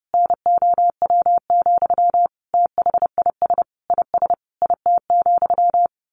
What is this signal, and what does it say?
Morse code, heard as a keyed tone, about 30 wpm, 700 Hz. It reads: NOW, THIS IS IT,
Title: A Sound Of Thunder 30wpm